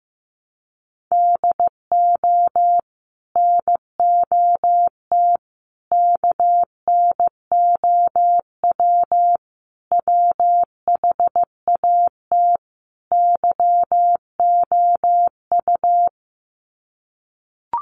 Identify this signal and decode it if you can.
Morse code, heard as a keyed tone, about 15 wpm, 700 Hz. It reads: DO NOT KNOW WHAT YOU
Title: Sets of 5 Words 15wpm